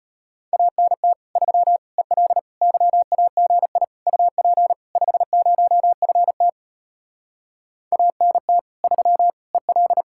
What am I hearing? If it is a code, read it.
Morse code, heard as a keyed tone, about 33 wpm, 700 Hz. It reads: ANT 3 EL YAGI UP 50FT ANT 3 EL